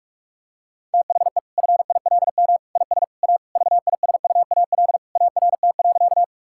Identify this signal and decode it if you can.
Morse code, heard as a keyed tone, about 41 wpm, 700 Hz. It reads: THE FILM IS A VISUAL ART.
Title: Sentences from Top 1000 Words - Review Part 6 - 45wpm